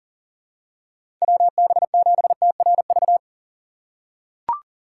Morse code, heard as a keyed tone, about 40 wpm, 700 Hz. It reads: WB7TRV
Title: USA Call Signs - 4x Character Spacing - 40wpm